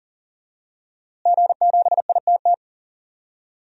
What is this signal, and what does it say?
Morse code, heard as a keyed tone, about 40 wpm, 700 Hz. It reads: G7ITT
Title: DXCC-100 Call Signs - 5 Characters - Repeated 2x - 40wpm